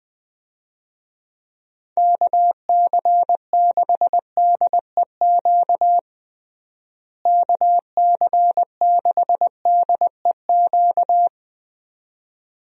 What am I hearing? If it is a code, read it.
Morse code, heard as a keyed tone, about 19 wpm, 700 Hz. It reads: KC6DEQ KC6DEQ
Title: USA Call Signs - 6 Characters - Repeated 2x - 20wpm